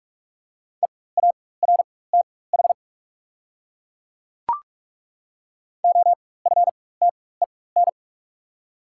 Morse code, heard as a keyed tone, about 43 wpm, spaced as for 21 wpm, 690 Hz. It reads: EARTH OFTEN
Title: 4x Character Spacing - Top 300 Words - 45wpm